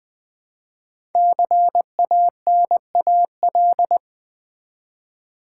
Morse code, heard as a keyed tone, about 20 wpm, 700 Hz. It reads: CANAL
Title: Build A Word - 1 Letter At A Time - 5-Letter Words - 20wpm